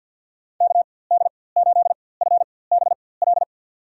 Morse code, heard as a keyed tone, about 53 wpm, spaced as for 24 wpm, 700 Hz. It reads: KD8FBL
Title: USA Call Signs - 6 Characters - 4x Character Spacing - 50wpm